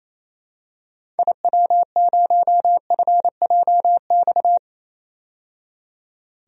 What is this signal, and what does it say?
Morse code, heard as a keyed tone, about 28 wpm, 700 Hz. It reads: IW0FJX
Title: DXCC-100 Call Signs - 6 Characters - 28wpm